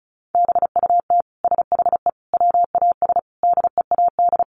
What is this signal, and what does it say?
Morse code, heard as a keyed tone, about 34 wpm, 700 Hz. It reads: BUT SHE WAS DEAD
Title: Sentences from Top 1500 Words 35wpm